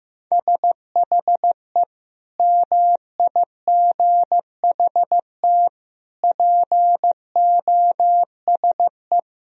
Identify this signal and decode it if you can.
Morse code, heard as a keyed tone, about 15 wpm, 700 Hz. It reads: SHE MIGHT POSE
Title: Sentences from Top 1000 Words - Review Part 2 - 15wpm